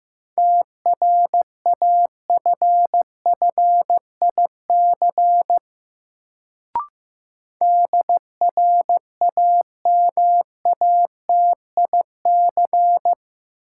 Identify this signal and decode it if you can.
Morse code, heard as a keyed tone, about 15 wpm, 700 Hz. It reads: TRAFFIC DRAMATIC
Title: Sentences from Top 1300 Words - New Word Drill - 15wpm